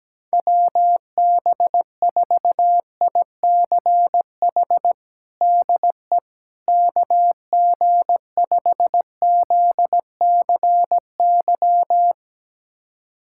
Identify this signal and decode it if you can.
Morse code, heard as a keyed tone, about 17 wpm, 700 Hz. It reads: WB4ICH DE KG5ZCY
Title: USA Callsign DE Callsign - 6 Characters - 17wpm